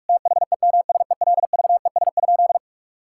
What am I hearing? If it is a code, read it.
Morse code, heard as a keyed tone, about 42 wpm, 700 Hz. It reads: THEMSELVES?
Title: Sentences from Top 1600 Words 45wpm